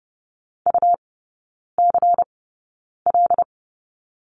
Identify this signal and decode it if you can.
Morse code, heard as a keyed tone, about 30 wpm, spaced as for 9 wpm, 700 Hz. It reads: UCL
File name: Lesson 19e - Letter w - Three Characters 30wpm